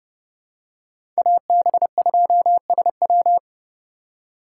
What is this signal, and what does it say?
Morse code, heard as a keyed tone, about 30 wpm, 700 Hz. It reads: AB2SW
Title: USA Call Signs - 3x Character Spacing - 30wpm